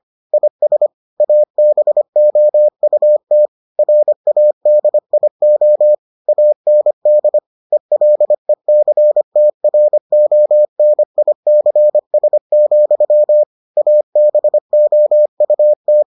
Morse code, heard as a keyed tone, about 25 wpm, 600 Hz. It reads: IS ABOUT RADIO AND ELECTRONICS, ABOUT